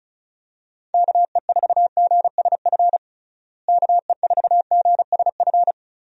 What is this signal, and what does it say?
Morse code, heard as a keyed tone, about 34 wpm, 700 Hz. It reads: KE4GSF KE4GSF
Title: USA Call Signs - 6 Characters - Repeated 2x - 35wpm